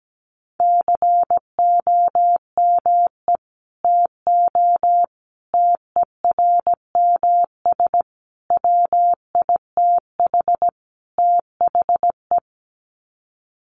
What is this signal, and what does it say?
Morse code, heard as a keyed tone, about 17 wpm, 700 Hz. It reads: COME TO TERMS WITH THE
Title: Sets of 5 Words Encore2 17wpm